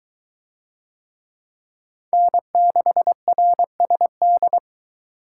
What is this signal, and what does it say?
Morse code, heard as a keyed tone, about 23 wpm, 700 Hz. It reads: N6RSD